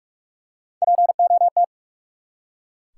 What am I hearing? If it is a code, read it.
Morse code, heard as a keyed tone, about 44 wpm, 700 Hz. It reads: POT
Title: Three Letter Words - 45wpm